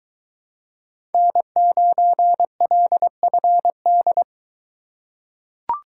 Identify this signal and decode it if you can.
Morse code, heard as a keyed tone, about 20 wpm, 700 Hz. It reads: N9LFD